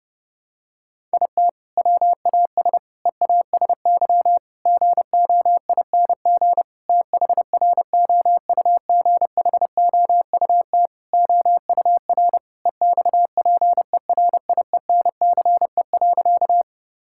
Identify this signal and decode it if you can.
Morse code, heard as a keyed tone, about 30 wpm, 700 Hz. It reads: IT WAS EASY GOING THROUGHOUT OUR EXPERIENCE.
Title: Sentences from Top 1000 Words - Review Part 5 - 30wpm